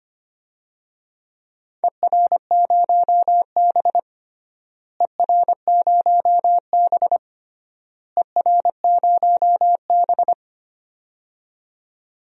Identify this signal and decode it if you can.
Morse code, heard as a keyed tone, about 25 wpm, 700 Hz. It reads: ER0B ER0B ER0B
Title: DXCC-100 Call Signs - 4 Characters - Repeated 3x - 25wpm